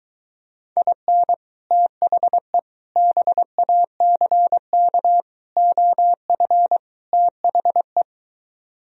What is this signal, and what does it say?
Morse code, heard as a keyed tone, about 23 wpm, 700 Hz. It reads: IN THE BACK OF THE